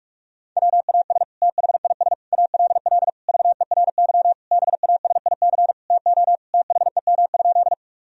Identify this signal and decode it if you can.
Morse code, heard as a keyed tone, about 47 wpm, 700 Hz. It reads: WAS THIS ALL VERY BASIC TO THEM?
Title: Sentences from Top 1000 Words - Review Part 3 - 45wpm